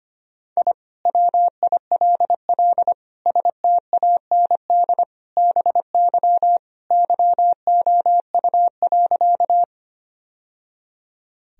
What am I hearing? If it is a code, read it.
Morse code, heard as a keyed tone, about 25 wpm, 700 Hz. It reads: I WILL STAND BY YOU.
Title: Sentences from Top 1000 Words - Review Part 1 - 25wpm